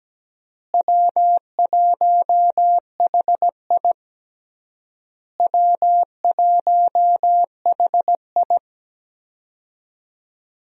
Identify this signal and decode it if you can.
Morse code, heard as a keyed tone, about 17 wpm, 700 Hz. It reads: W1HI W1HI
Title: USA Call Signs - 4 Characters - Repeated 3x - 17wpm